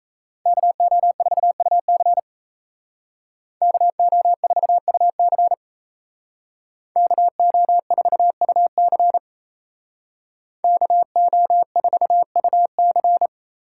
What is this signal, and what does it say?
Morse code, heard as a keyed tone, about 37 wpm, 700 Hz. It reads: KO4UC KO4UC KO4UC KO4UC